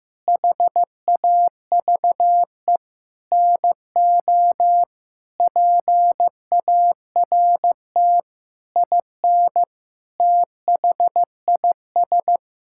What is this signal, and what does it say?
Morse code, heard as a keyed tone, about 15 wpm, 700 Hz. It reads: HAVE NO PART IN THIS
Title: Sentences from Top 1000 Words - Review Part 5 - 15wpm